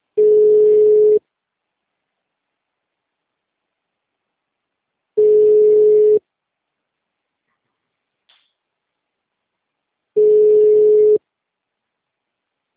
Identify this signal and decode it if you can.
Ringback tone (Europe, 425 Hz).